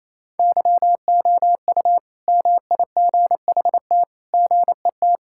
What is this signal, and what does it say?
Morse code, heard as a keyed tone, about 28 wpm, 700 Hz. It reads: YOU MIGHT GET
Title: Sentences from Top 1000 Words - Review Part 5 - 28wpm